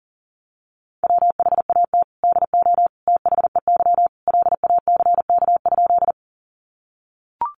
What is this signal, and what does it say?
Morse code, heard as a keyed tone, about 40 wpm, 700 Hz. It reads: WHAT DO THEY LACK?
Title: Sentences from Top 1100 Words 40wpm